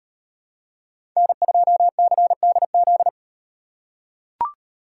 Morse code, heard as a keyed tone, about 37 wpm, 700 Hz. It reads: N2CDZ